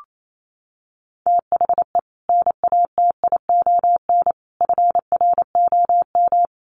Morse code, heard as a keyed tone, about 28 wpm, 700 Hz. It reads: THE NATION FROM
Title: Sets of 5 Words 28wpm